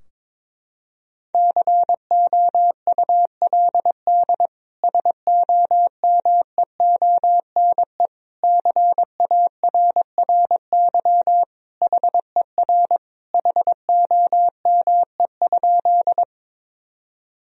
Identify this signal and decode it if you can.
Morse code, heard as a keyed tone, about 22 wpm, 700 Hz. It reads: COULD SOMEONE CARRY HER HOME?
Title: Sentences from Top 1000 Words - Review Part 4 - 22wpm